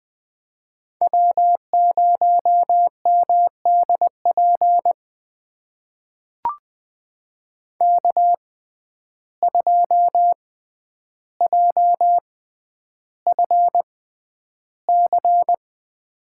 Morse code, heard as a keyed tone, about 20 wpm, spaced as for 10 wpm, 700 Hz. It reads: W0MDP K2JFC
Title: USA Call Signs - 5 Characters - 6x Character Spacing - 20wpm